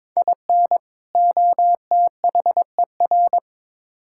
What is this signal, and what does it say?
Morse code, heard as a keyed tone, about 21 wpm, 700 Hz. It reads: IN OTHER
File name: practice / 22wpm / Build A Phrase - 5-Word Phrases - 22wpm